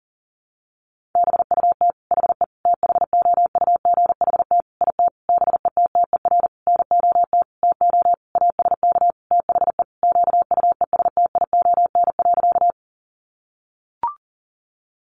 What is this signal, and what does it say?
Morse code, heard as a keyed tone, about 40 wpm, 700 Hz. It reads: BUT HE THOUGHT IT BETTER NOT TO ASK THE QUESTION.
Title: Sentences from Top 600 Words 40wpm